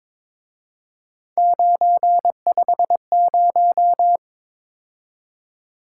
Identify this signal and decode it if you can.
Morse code, heard as a keyed tone, about 26 wpm, 700 Hz. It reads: 950
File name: Speed-Racing - Three Letter-Number Combinations - 22wpm